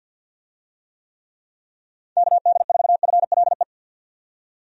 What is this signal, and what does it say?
Morse code, heard as a keyed tone, about 51 wpm, 700 Hz. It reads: KD4FLE
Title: USA Call Signs - 6 Characters - 50wpm